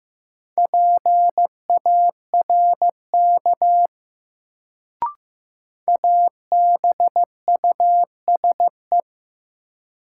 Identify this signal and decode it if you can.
Morse code, heard as a keyed tone, about 15 wpm, 700 Hz. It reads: PARK ABUSE
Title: Sentences from Top 1300 Words - New Word Drill - 15wpm